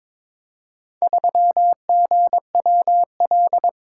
Morse code, heard as a keyed tone, about 22 wpm, 700 Hz. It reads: 3GWL
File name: practice / 22wpm / Four Letter-Number Combinations - 22wpm